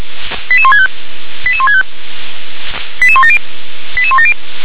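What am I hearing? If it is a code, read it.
Five-tone selective call (ZVEI), heard as quick runs of five tones, tones of about 70 ms. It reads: 88165 88165 88170 88170
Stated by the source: ZVEI five-tone selective call received on SDR.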